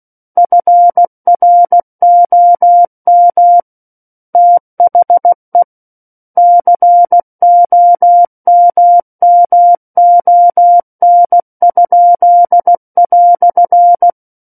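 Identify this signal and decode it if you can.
Morse code, heard as a keyed tone, about 16 wpm, 700 Hz. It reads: FROM THE COMMON?"
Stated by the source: Morse practice of book text at 16 wpm.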